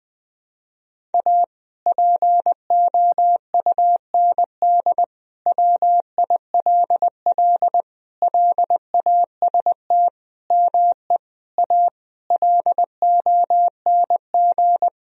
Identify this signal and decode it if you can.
Morse code, heard as a keyed tone, about 19 wpm, 700 Hz. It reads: A POUND WILL LAST ME A LONG
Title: Sentences from Top 1000 Words - Review Part 2 - 20wpm